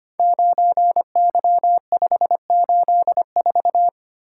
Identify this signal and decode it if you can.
Morse code, heard as a keyed tone, about 25 wpm, 700 Hz. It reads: 9Y584
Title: Code Groups - 5 Letter:Numbers - 25wpm